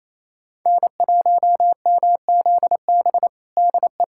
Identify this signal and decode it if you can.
Morse code, heard as a keyed tone, about 28 wpm, 700 Hz. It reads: N1MZB DE
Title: USA Callsign DE Callsign - 5 Characters - 28wpm